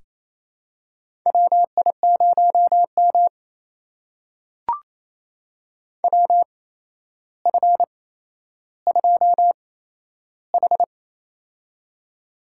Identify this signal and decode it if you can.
Morse code, heard as a keyed tone, about 28 wpm, spaced as for 12 wpm, 700 Hz. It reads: WI0M WF2H
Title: USA Call Signs - 4 Characters - 8x Character Spacing - 28wpm